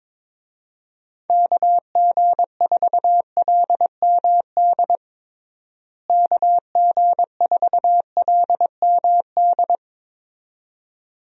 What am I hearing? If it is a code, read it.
Morse code, heard as a keyed tone, about 22 wpm, 700 Hz. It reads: KG4LMD KG4LMD
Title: USA Call Signs - 6 Characters - Repeated 2x - 22wpm